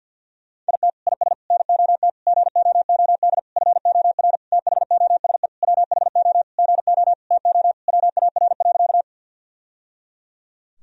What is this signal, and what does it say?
Morse code, heard as a keyed tone, about 53 wpm, 700 Hz. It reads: IT IS NOT GOOD FOR THOSE WHO GO TO WAR.
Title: Sentences from Top 1000 Words - Review Part 2 - 50wpm